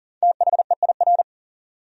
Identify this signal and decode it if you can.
Morse code, heard as a keyed tone, about 40 wpm, 700 Hz. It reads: THEIR
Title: Top 100 Words 40wpm